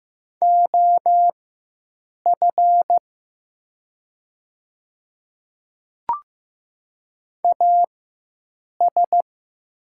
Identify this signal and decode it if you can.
Morse code, heard as a keyed tone, about 15 wpm, spaced as for 7 wpm, 700 Hz. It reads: OF AS
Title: 4x Character Spacing - Top 100 Words - 15wpm